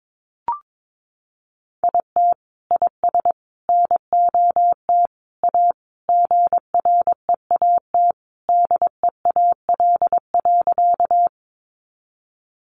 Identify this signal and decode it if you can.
Morse code, heard as a keyed tone, about 22 wpm, 700 Hz. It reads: IT IS NOT A GREAT DEAL.
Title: Sentences from Top 800 Words 22wpm